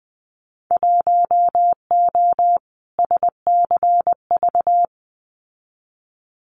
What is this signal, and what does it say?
Morse code, heard as a keyed tone, about 20 wpm, 700 Hz. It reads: 1O SCV
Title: Winter Field Day Exchanges 20wpm